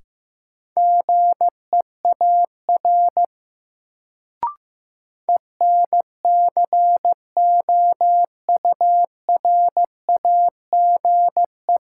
Morse code, heard as a keyed tone, about 15 wpm, 700 Hz. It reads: GEAR ENCOURAGE
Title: Sentences from Top 1400 Words - New Word Drill - 15wpm